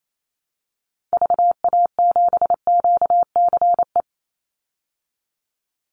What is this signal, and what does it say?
Morse code, heard as a keyed tone, about 28 wpm, 700 Hz. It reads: VA7QCE
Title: Call Signs Repeated 3x - 28wpm